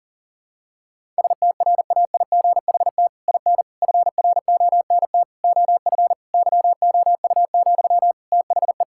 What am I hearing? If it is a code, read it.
Morse code, heard as a keyed tone, about 37 wpm, 700 Hz. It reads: STRAIGHT IN FRONT OF YOU, THE
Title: Sentences from Top 1000 Words - Review Part 6 - 40wpm